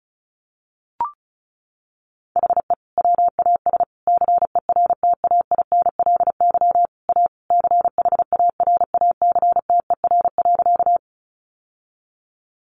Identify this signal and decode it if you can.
Morse code, heard as a keyed tone, about 34 wpm, 700 Hz. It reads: HE WAS CERTAINLY A CHARACTER.
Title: Sentences from Top 1200 Words 35wpm